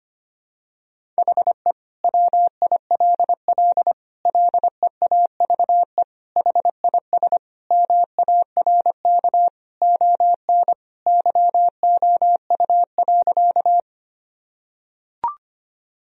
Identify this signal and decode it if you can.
Morse code, heard as a keyed tone, about 25 wpm, 700 Hz. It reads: HE WILL LEAVE HIS MARK ON YOU.
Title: Sentences from Top 1000 Words - Review Part 1 - 25wpm